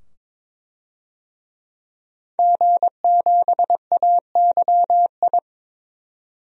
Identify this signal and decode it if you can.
Morse code, heard as a keyed tone, about 22 wpm, 700 Hz. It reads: G7AYI
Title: DXCC-100 Call Signs - 5 Characters - 22wpm